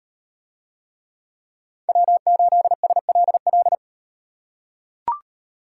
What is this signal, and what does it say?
Morse code, heard as a keyed tone, about 38 wpm, 700 Hz. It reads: W8SLL